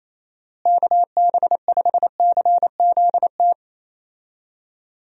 Morse code, heard as a keyed tone, about 27 wpm, 700 Hz. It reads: KB5CZT
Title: USA Call Signs - 6 Characters - Repeated 3x - 28wpm